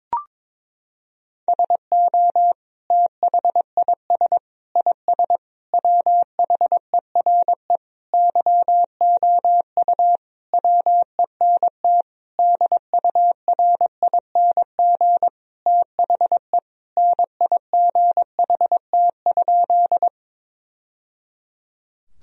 Morse code, heard as a keyed tone, about 22 wpm, 700 Hz. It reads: SO THIS IS WHERE YOU WENT DURING THE NIGHT?
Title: Sentences from Top 1000 Words - Review Part 2 - 22wpm